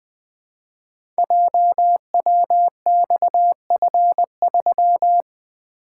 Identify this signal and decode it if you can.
Morse code, heard as a keyed tone, about 20 wpm, 700 Hz. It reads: JWXF3
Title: Code Groups - 5 Letter:Numbers - 20wpm